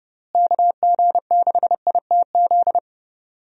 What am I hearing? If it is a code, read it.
Morse code, heard as a keyed tone, about 30 wpm, 700 Hz. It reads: KG6ITZ
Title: USA Call Signs - 6 Characters - Repeated 2x - 30wpm